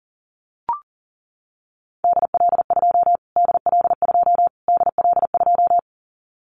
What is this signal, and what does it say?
Morse code, heard as a keyed tone, about 40 wpm, 700 Hz. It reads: DL2 DL2 DL2
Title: Callsign Prefixes for Top 25 DXCC - Repeated 3x - 40wpm